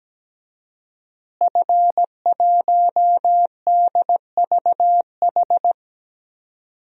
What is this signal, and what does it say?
Morse code, heard as a keyed tone, about 17 wpm, 700 Hz. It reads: F1DVH
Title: DXCC-100 Call Signs - 5 Characters - Repeated 2x - 17wpm